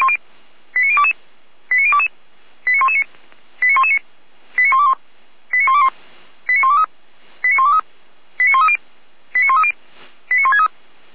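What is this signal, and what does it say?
Five-tone selective call (ZVEI), heard as quick runs of five tones, tones of about 70 ms. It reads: 89022 89022 89119 89119 89121 89121 89123 89123 89130 89130 89163
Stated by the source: ZVEI five-tone call received by SDR, 455.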